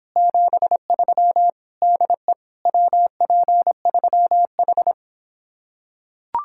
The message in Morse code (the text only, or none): 73 DE WP3H